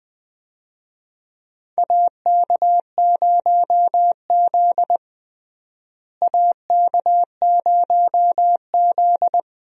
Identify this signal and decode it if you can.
Morse code, heard as a keyed tone, about 20 wpm, 700 Hz. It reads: AK0Z AK0Z
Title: USA Call Signs - 4 Characters - Repeated 2x - 20wpm